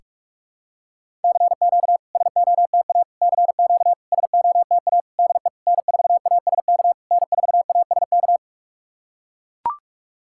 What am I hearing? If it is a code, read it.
Morse code, heard as a keyed tone, about 44 wpm, 690 Hz. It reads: CQ SOTA CQ SOTA DE N4ASK N4ASK
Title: SOTA QSO Practice 45wpm